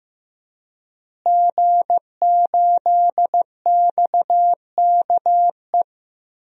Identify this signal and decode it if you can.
Morse code, heard as a keyed tone, about 15 wpm, 700 Hz. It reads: G8XKE
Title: DXCC-100 Call Signs - 5 Characters - Repeated 2x - 15wpm